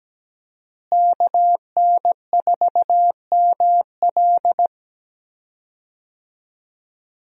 Morse code, heard as a keyed tone, about 17 wpm, 700 Hz. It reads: KN4ML
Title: USA Call Signs - 5 Characters - Repeated 2x - 17wpm